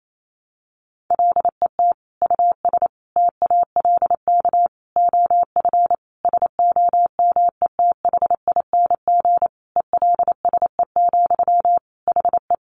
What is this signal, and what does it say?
Morse code, heard as a keyed tone, about 28 wpm, 700 Hz. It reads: LET US TALK OF SOMETHING ELSE, HE
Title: Sentences from Top 800 Words 28wpm